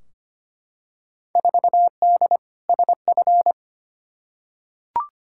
Morse code, heard as a keyed tone, about 25 wpm, 700 Hz. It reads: 4D SF
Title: Field Day Exchanges Encore - 25wpm